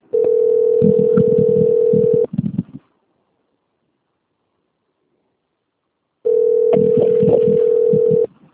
Ringback tone (North America), 440 + 480 Hz.
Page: Itatiba II